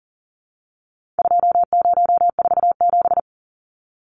Morse code, heard as a keyed tone, about 40 wpm, 700 Hz. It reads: 2047
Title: Numbers - 4-Digit 40wpm